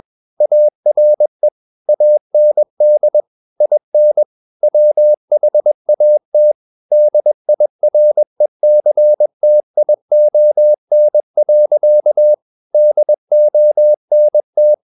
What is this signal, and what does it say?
Morse code, heard as a keyed tone, about 21 wpm, 600 Hz. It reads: ARE AND IN WHAT DIRECTION. DONT
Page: CW med Gnister